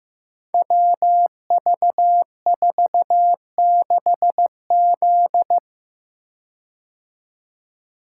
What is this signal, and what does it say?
Morse code, heard as a keyed tone, about 15 wpm, 700 Hz. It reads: WV46Z
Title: Code Groups - 5 Letter:Numbers - 15wpm